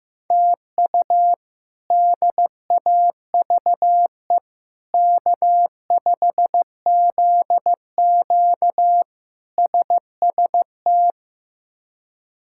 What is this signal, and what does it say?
Morse code, heard as a keyed tone, about 15 wpm, 700 Hz. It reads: TU DAVE K5ZQ SST
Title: K1USN SST QSO Practice - 15wpm